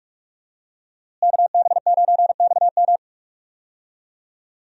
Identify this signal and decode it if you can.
Morse code, heard as a keyed tone, about 46 wpm, 700 Hz. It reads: KB9XM
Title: USA Call Signs - 5 Characters - Repeated 3x - 45wpm